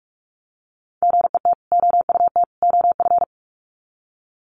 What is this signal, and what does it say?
Morse code, heard as a keyed tone, about 44 wpm, 700 Hz. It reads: GET OUT OF
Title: Sets of 3 Words 45wpm